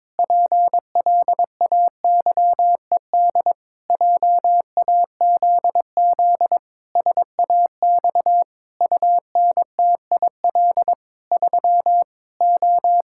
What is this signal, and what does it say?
Morse code, heard as a keyed tone, about 21 wpm, 700 Hz. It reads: PLAYED JAZZ SAX UNTIL 3 O
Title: Pangram Challenge v2 - 22wpm